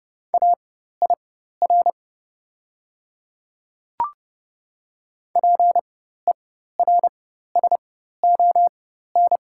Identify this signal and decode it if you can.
Morse code, heard as a keyed tone, about 30 wpm, spaced as for 14 wpm, 690 Hz. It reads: AIR PERSON
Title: 4x Character Spacing - Top 500 Words - 30wpm